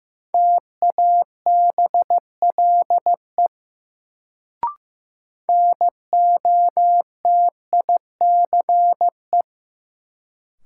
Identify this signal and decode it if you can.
Morse code, heard as a keyed tone, about 15 wpm, 700 Hz. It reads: TABLE NOTICE
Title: Sentences from Top 400 Words - New Word Drill - 15wpm